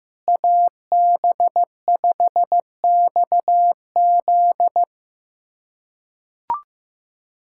Morse code, heard as a keyed tone, about 15 wpm, 700 Hz. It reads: AB5XZ
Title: USA Callsign DE Callsign 15wpm